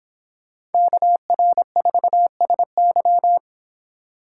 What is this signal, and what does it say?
Morse code, heard as a keyed tone, about 25 wpm, 700 Hz. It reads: KR4SY